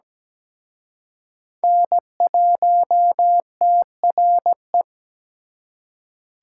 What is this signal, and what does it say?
Morse code, heard as a keyed tone, about 20 wpm, 700 Hz. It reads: N1TRE
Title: Speed Racing - USA 1x3 Call Signs 17wpm